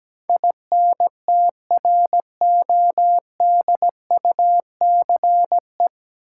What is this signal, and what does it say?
Morse code, heard as a keyed tone, about 17 wpm, 700 Hz. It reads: INTRODUCE
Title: Sentences from Top 1600 Words - New Word Drill Alt - 17wpm